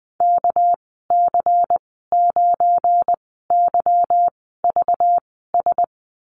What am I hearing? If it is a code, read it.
Morse code, heard as a keyed tone, about 20 wpm, spaced as for 14 wpm, 700 Hz. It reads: KC9YVS
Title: 238 Lesson 38g - Letter q - Call Signs 20wpm